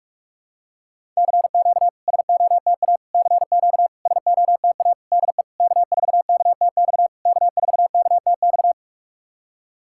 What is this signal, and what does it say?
Morse code, heard as a keyed tone, about 46 wpm, 690 Hz. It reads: CQ SOTA CQ SOTA DE K4KTX K4KTX
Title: SOTA QSO Practice 45wpm